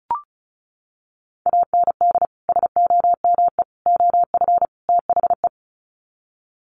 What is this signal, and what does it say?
Morse code, heard as a keyed tone, about 34 wpm, 700 Hz. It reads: AND SOME OF THE
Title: Sets of 4 Words 35wpm